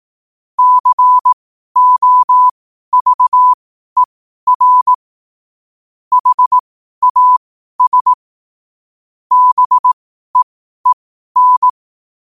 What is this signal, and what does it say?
Morse code, heard as a keyed tone, about 18 wpm, spaced as for 12 wpm, 1000 Hz. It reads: COVER HAS BEEN
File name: qotdfiles / 12wpm